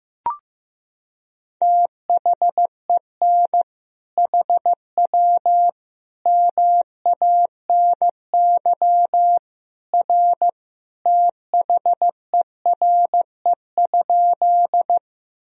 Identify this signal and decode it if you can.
Morse code, heard as a keyed tone, about 16 wpm, 700 Hz. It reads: THEN HW MANY R THERE?
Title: Sentences from Top 100 Words and Abbreviations 15wpm